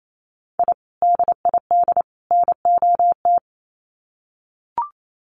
Morse code, heard as a keyed tone, about 28 wpm, 700 Hz. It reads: I DID NOT
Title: Sets of 3 Words 28wpm